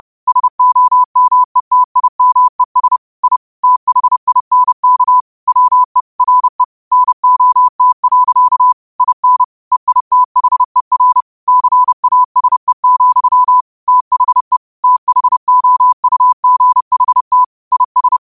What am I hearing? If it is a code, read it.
Morse code, heard as a keyed tone, about 30 wpm, 1000 Hz. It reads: SOMETIMES I THINK WE RE NOT. IN EITHER CASE, THE THOUGHT IS